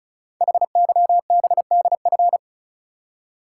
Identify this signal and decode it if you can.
Morse code, heard as a keyed tone, about 36 wpm, 690 Hz. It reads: HYBDF
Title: Code Groups - 5 Letter:Numbers - 35wpm